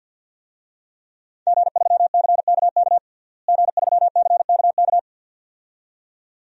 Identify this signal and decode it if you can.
Morse code, heard as a keyed tone, about 46 wpm, 700 Hz. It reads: G3CKK G3CKK
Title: DXCC-100 Call Signs - 5 Characters - Repeated 2x - 50wpm